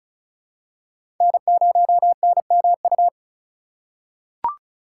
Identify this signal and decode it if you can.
Morse code, heard as a keyed tone, about 34 wpm, 700 Hz. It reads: N0NMU
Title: USA Call Signs - 5 Characters - 3x Character Spacing - 35wpm